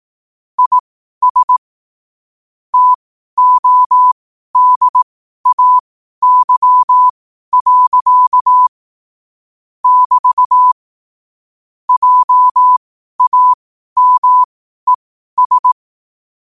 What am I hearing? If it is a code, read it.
Morse code, heard as a keyed tone, about 18 wpm, spaced as for 12 wpm, 1000 Hz. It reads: IS TODAY. <BT> JAMES
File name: qotdfiles / 12wpm